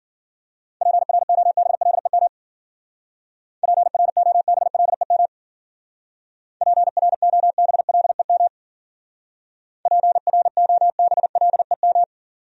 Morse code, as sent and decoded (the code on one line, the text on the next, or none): .--. .-. --- -... .-.. -- / .--. .-. --- -... .-.. -- / .--. .-. --- -... .-.. -- / .--. .-. --- -... .-.. --
PROBLEM PROBLEM PROBLEM PROBLEM